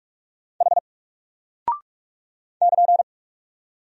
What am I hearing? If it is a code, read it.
Morse code, heard as a keyed tone, about 46 wpm, 700 Hz. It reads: H <KN>
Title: Single Character (Letter:Number:Punctuation:Prosign) 45wpm